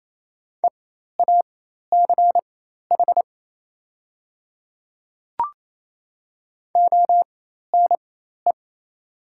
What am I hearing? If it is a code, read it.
Morse code, heard as a keyed tone, about 27 wpm, spaced as for 13 wpm, 700 Hz. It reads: EACH ONE
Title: 4x Character Spacing - Top 300 Words - 28wpm